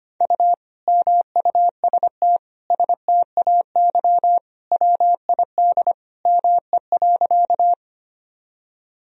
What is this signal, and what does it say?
Morse code, heard as a keyed tone, about 25 wpm, 700 Hz. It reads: U MUST STAY WID ME.
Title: Sentences from Top 500 Words and Abbreviations 25wpm